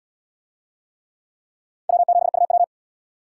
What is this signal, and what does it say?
Morse code, heard as a keyed tone, about 69 wpm, 700 Hz. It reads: W7UW